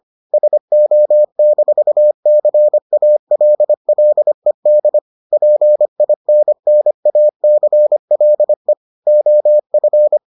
Morse code, heard as a keyed tone, about 25 wpm, 600 Hz. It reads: SO-CALLED PINNACLE OF